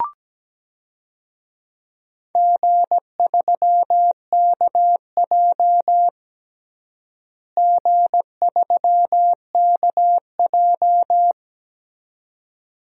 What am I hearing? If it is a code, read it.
Morse code, heard as a keyed tone, about 17 wpm, 700 Hz. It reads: G3KJ G3KJ
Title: DXCC-100 Call Signs - 4 Characters - Repeated 2x - 17wpm